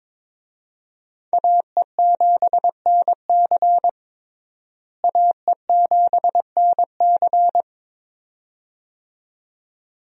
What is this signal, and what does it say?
Morse code, heard as a keyed tone, about 22 wpm, 700 Hz. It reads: AE7NC AE7NC
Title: USA Call Signs - 5 Characters - Repeated 2x - 22wpm